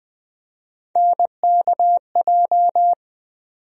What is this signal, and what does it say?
Morse code, heard as a keyed tone, about 20 wpm, 700 Hz. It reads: NKJ
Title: USA Call Sign Postfixes - 3 Letters 20wpm